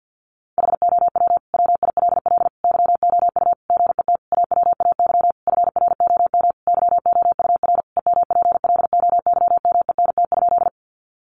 Morse code, heard as a keyed tone, about 50 wpm, 700 Hz. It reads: HOW WILL YOU GET AWAY FROM YOUR EMPLOYMENT?
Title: Sentences from Top 1500 Words 50wpm